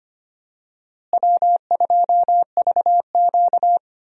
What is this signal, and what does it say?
Morse code, heard as a keyed tone, about 25 wpm, 700 Hz. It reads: W2VQ
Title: USA Call Signs - 4 Characters - Repeated 3x - 25wpm